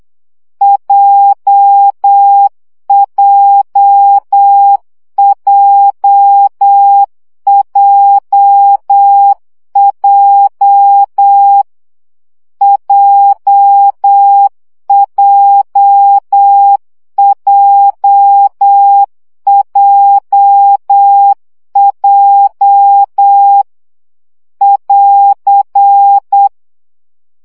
Morse code, heard as a keyed tone, about 8 wpm, 800 Hz. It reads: JJJJJ JJJJJ <AR>
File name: CW gyakorlas j 7 baud